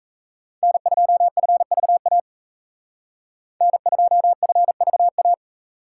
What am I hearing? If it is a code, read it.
Morse code, heard as a keyed tone, about 36 wpm, 700 Hz. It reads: N2FVA N2FVA